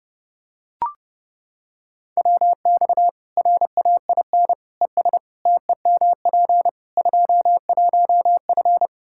Morse code, heard as a keyed tone, about 32 wpm, 700 Hz. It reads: WX RAIN ES TEMP 21F
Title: Practice QSO - Exchange 2 - Info - 30wpm